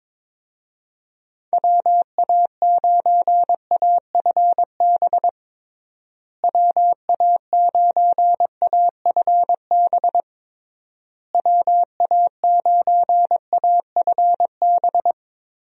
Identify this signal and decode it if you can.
Morse code, heard as a keyed tone, about 22 wpm, 700 Hz. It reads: WA9AFB WA9AFB WA9AFB
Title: USA Call Signs - 6 Characters - Repeated 3x - 22wpm